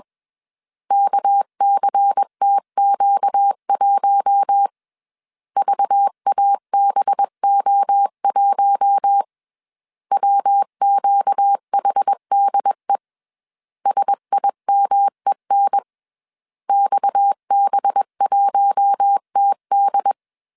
Morse code, heard as a keyed tone, about 21 wpm, 800 Hz. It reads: KCTQ1 VABO1 WQHDE SIMEN XB1TD
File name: CW / DET / 1 6 1500C S B 20wpm 2048